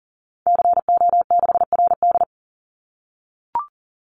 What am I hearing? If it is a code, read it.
Morse code, heard as a keyed tone, about 40 wpm, 700 Hz. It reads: CO6RD
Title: Call Signs - 40wpm